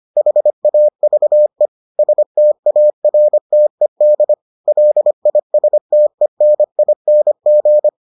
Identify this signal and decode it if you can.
Morse code, heard as a keyed tone, about 25 wpm, 600 Hz. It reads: HAVE STARTED LISTENING